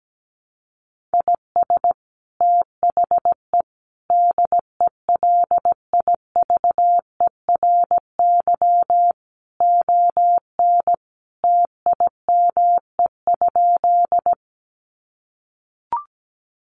Morse code, heard as a keyed tone, about 17 wpm, 700 Hz. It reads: IS THE DELIVERY ON TIME?
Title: Sentences from Top 1500 Words 17wpm